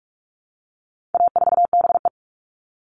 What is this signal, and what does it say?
Morse code, heard as a keyed tone, about 44 wpm, 700 Hz. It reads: A4BE
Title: All Cut Numbers - 4-Digit 45wpm